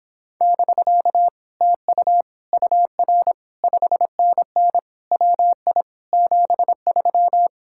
Morse code, heard as a keyed tone, about 26 wpm, 700 Hz. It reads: <BK> TU UR 5NN WI 73